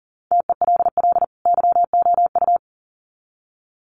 Morse code, heard as a keyed tone, about 40 wpm, 700 Hz. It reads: TELL YOU
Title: Sets of 2 Words 40wpm